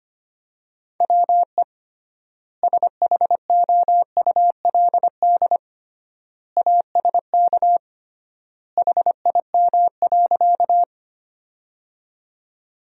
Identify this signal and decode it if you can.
Morse code, heard as a keyed tone, about 25 wpm, 700 Hz. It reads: WE SHOULD ASK HIM.
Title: 3x Word-Spacing - Sentences from Top 300 Words - 25wpm